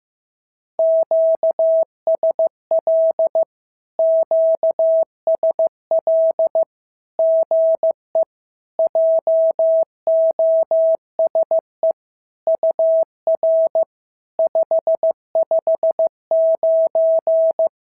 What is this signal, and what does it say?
Morse code, heard as a keyed tone, about 14 wpm, 650 Hz. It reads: QSL QSL GE JOSE UR 559
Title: SOTA QSO Practice 15wpm